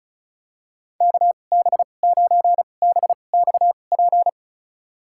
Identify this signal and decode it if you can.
Morse code, heard as a keyed tone, about 36 wpm, spaced as for 25 wpm, 700 Hz. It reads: KB9BXP
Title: USA Call Signs - 6 Characters - 2x Character Spacing - 35wpm